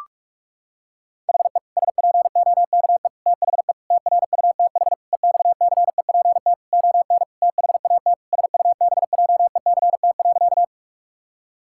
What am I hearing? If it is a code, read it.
Morse code, heard as a keyed tone, about 46 wpm, 700 Hz. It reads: HE SPOKE THE TRUTH EXCEPT ON THAT SUBJECT.
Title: Sentences from Top 1600 Words 45wpm